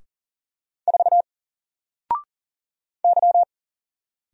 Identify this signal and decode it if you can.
Morse code, heard as a keyed tone, about 41 wpm, 700 Hz. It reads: 4 Y
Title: Single Letter-Number Alt 40wpm